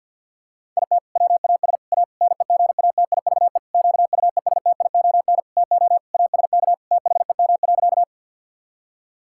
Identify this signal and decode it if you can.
Morse code, heard as a keyed tone, about 48 wpm, 700 Hz. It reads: IT WAS A NEGATIVE QUESTION TO ASK THEM.
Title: Sentences from Top 1000 Words - Review Part 6 - 50wpm